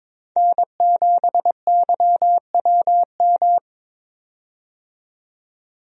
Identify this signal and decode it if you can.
Morse code, heard as a keyed tone, about 22 wpm, 700 Hz. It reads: N7YWM
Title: USA Call Signs - 5 Characters - 22wpm